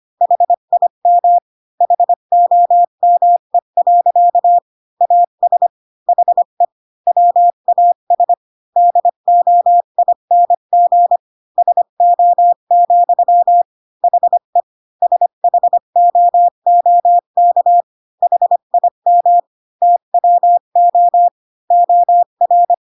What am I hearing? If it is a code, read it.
Morse code, heard as a keyed tone, about 25 wpm, 700 Hz. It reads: HIM HOME. AS HE WAS DOING SO, HE SHOOK HIM TWO OR
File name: Pinocchio / Pinocchio ch3 20 25 ik1hge